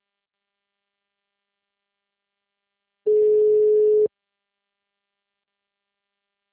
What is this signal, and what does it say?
Ringback tone (Europe, 425 Hz).
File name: out- -2030- - - .34